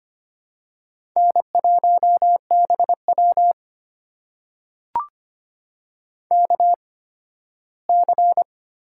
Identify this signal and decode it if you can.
Morse code, heard as a keyed tone, about 25 wpm, spaced as for 15 wpm, 700 Hz. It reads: N1BW KC
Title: USA Call Signs - 8x Character Spacing - 25wpm